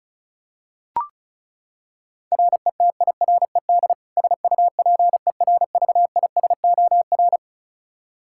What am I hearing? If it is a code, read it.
Morse code, heard as a keyed tone, about 34 wpm, 700 Hz. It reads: RETIRED SUPERVISOR
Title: K1USN SST QSO Practice 35wpm